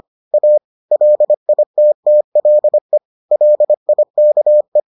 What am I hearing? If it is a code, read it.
Morse code, heard as a keyed tone, about 25 wpm, 600 Hz. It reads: A LITTLE LIKE